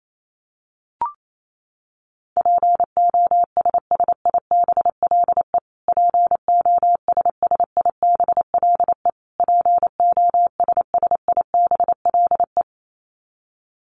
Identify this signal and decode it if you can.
Morse code, heard as a keyed tone, about 28 wpm, 700 Hz. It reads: POSSIBLE POSSIBLE POSSIBLE
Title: Top 500 Words Repeated 28wpm